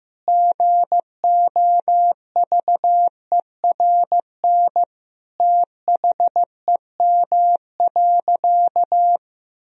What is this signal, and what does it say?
Morse code, heard as a keyed tone, about 16 wpm, 700 Hz. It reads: GOVERN THEM.
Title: Sentences from Top 1000 Words - Review Part 2 - 15wpm